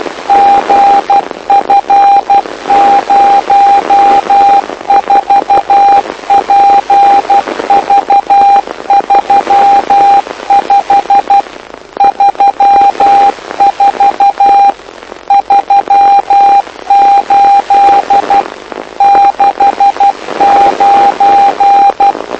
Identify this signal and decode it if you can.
Morse code, heard as a keyed tone, about 12 wpm, 800 Hz, with radio noise. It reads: GF04PV35 34 38 69
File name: lu4aao / lu7aa 09dic2011 exp planeador vuelo2